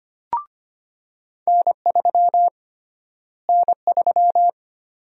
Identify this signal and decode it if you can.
Morse code, heard as a keyed tone, about 25 wpm, 700 Hz. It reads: N3 N3
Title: USA Call Sign Prefixes - Repeated 2x - 25wpm